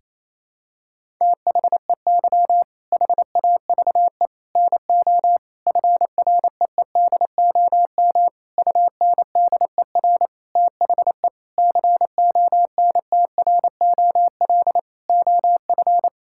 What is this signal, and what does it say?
Morse code, heard as a keyed tone, about 27 wpm, 700 Hz. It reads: THEY HAVE NO FREEDOM UNDER THE CONTROL OF
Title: Sentences from Top 1000 Words - Review Part 5 - 28wpm